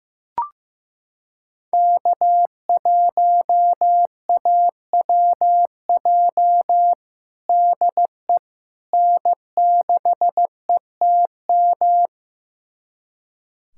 Morse code, heard as a keyed tone, about 15 wpm, 700 Hz. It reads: K1AWJ DE N6ETM
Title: USA Callsign DE Callsign 15wpm